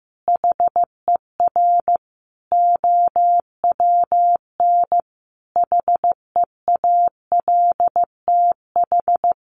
Morse code, heard as a keyed tone, about 15 wpm, 700 Hz. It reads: HER OWN HEALTH
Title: Sentences from Top 1100 Words 15wpm